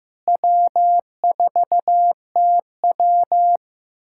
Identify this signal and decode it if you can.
Morse code, heard as a keyed tone, about 15 wpm, 700 Hz. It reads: W4TW
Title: USA Callsign DE Callsign - 4 Characters - 15wpm